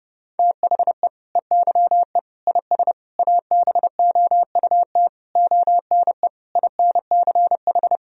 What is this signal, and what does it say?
Morse code, heard as a keyed tone, about 32 wpm, 700 Hz. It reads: THE EYE IS ABOUT ONE INCH
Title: Sentences from Top 1000 Words - Review Part 3 - 30wpm